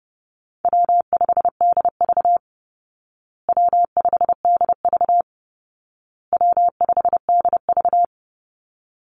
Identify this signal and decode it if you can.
Morse code, heard as a keyed tone, about 30 wpm, 700 Hz. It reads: W5DV W5DV W5DV
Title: Call Signs Repeated 2x - 30wpm